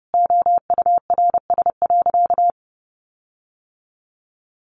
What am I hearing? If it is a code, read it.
Morse code, heard as a keyed tone, about 30 wpm, 700 Hz. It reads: OURS.
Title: Sentences from Top 1300 Words 30wpm